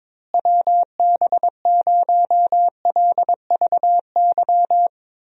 Morse code, heard as a keyed tone, about 22 wpm, 700 Hz. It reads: WB0LVY
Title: USA Call Signs - 6 Characters - Repeated 2x - 22wpm